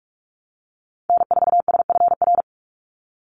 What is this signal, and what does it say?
Morse code, heard as a keyed tone, about 46 wpm, 700 Hz. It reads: N4SFR
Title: NA SOTA Activator Callsigns 45wpm